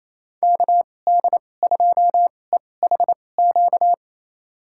Morse code, heard as a keyed tone, about 28 wpm, spaced as for 20 wpm, 700 Hz. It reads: KD2EHQ
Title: USA Call Signs - 2x Character Spacing - 28wpm